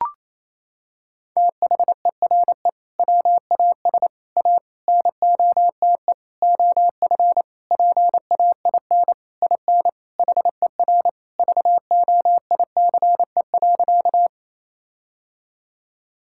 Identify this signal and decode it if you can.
Morse code, heard as a keyed tone, about 27 wpm, 700 Hz. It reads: THERE WAS A NOTE OF PAIN IN HER VOICE.
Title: Sentences from Top 1600 Words 28wpm